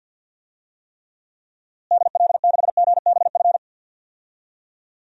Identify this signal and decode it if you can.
Morse code, heard as a keyed tone, about 52 wpm, 700 Hz. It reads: DL6GBF
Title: DXCC-100 Call Signs - 6 Characters - Repeated 2x - 50wpm